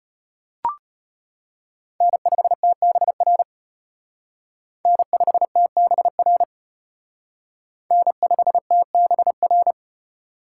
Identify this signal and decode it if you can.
Morse code, heard as a keyed tone, about 32 wpm, 700 Hz. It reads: N5TBR N5TBR N5TBR